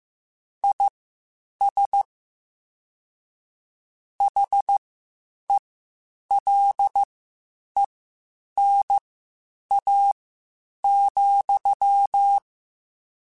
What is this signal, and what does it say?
Morse code, heard as a keyed tone, about 15 wpm, spaced as for 8 wpm, 800 Hz. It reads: IS HELENA,